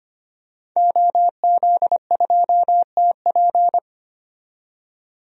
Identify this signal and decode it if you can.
Morse code, heard as a keyed tone, about 25 wpm, 700 Hz. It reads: OZ2TP
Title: DXCC-100 Call Signs - 5 Characters - 25wpm